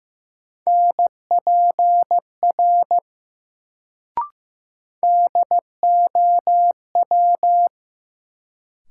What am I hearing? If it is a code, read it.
Morse code, heard as a keyed tone, about 15 wpm, 700 Hz. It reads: NPR DOW
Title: Sets of 5 Words - New Word Drill - 15wpm